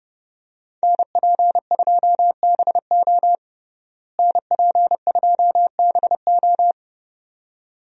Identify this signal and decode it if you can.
Morse code, heard as a keyed tone, about 30 wpm, 700 Hz. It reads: NP2BO NP2BO
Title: USA Call Signs - 5 Characters - Repeated 2x - 30wpm